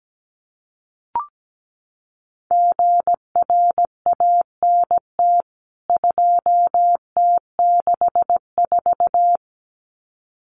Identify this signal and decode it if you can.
Morse code, heard as a keyed tone, about 17 wpm, 700 Hz. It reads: GRANT 2T64
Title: CWT Exchange 17wpm